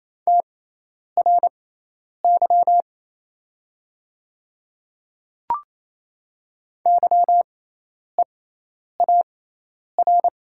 Morse code, heard as a keyed tone, about 27 wpm, spaced as for 10 wpm, 700 Hz. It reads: TRY YEAR
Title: 6x Character Spacing - Top 200 Words - 28wpm